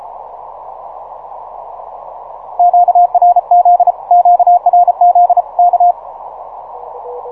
Morse code, heard as a keyed tone, about 33 wpm, 710 Hz, with radio noise. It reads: QRZ QRZ K
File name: simple-file-list / 5A1AL 6MCW